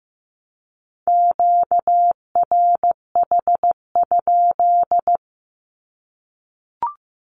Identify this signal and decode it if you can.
Morse code, heard as a keyed tone, about 15 wpm, 700 Hz. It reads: QRH?
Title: Q-Codes 15wpm